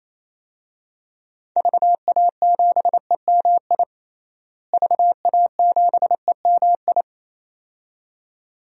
Morse code, heard as a keyed tone, about 28 wpm, 700 Hz. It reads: VA7EMI VA7EMI
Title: DXCC-100 Call Signs - 6 Characters - Repeated 2x - 28wpm